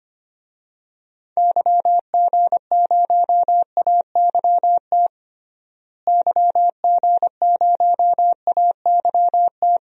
Morse code, heard as a keyed tone, about 25 wpm, 700 Hz. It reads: YG0AYT YG0AYT
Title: DXCC-100 Call Signs - 6 Characters - Repeated 3x - 25wpm